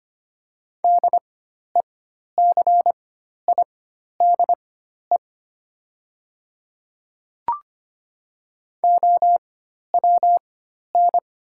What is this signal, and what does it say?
Morse code, heard as a keyed tone, about 25 wpm, spaced as for 12 wpm, 700 Hz. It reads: DECIDE OWN
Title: 4x Character Spacing - Top 500 Words - 25wpm